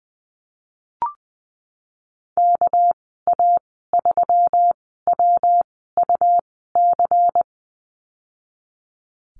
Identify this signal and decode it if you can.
Morse code, heard as a keyed tone, about 20 wpm, spaced as for 15 wpm, 700 Hz. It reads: KA3WUC
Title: 219 Lesson 35g - Number 0 - Call Signs 20wpm